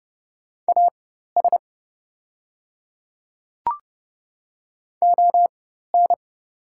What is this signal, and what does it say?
Morse code, heard as a keyed tone, about 30 wpm, spaced as for 14 wpm, 700 Hz. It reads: AS ON
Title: 4x Character Spacing - Top 100 Words - 30wpm